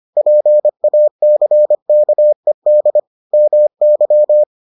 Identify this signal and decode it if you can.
Morse code, heard as a keyed tone, about 25 wpm, 600 Hz. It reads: PACKED MY